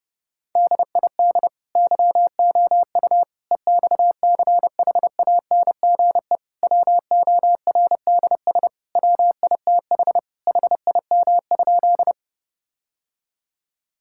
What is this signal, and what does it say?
Morse code, heard as a keyed tone, about 30 wpm, 700 Hz. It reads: DID YOU EXCHANGE WORDS WITH HIM?
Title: Sentences from Top 1000 Words - Review Part 6 - 30wpm